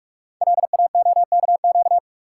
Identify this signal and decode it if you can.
Morse code, heard as a keyed tone, about 44 wpm, 700 Hz. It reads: LAOKQ
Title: Code Groups - 5 Letters - 45wpm